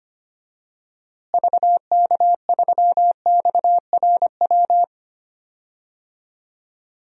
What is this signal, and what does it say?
Morse code, heard as a keyed tone, about 25 wpm, 700 Hz. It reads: VK3XRW
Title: DXCC-100 Call Signs - 6 Characters - Repeated 2x - 25wpm